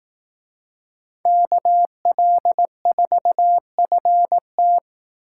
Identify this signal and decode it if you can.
Morse code, heard as a keyed tone, about 18 wpm, 700 Hz. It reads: KL4FT
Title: Speed Racing - USA Call Signs 15wpm